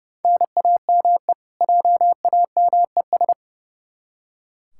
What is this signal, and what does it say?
Morse code, heard as a keyed tone, about 32 wpm, 700 Hz. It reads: NAME JAMES
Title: Practice QSO - Exchange 1 - Beginning - 30wpm